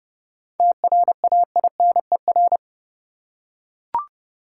Morse code, heard as a keyed tone, about 31 wpm, 700 Hz. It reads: TRAINER
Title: Seven Letter Words - 30wpm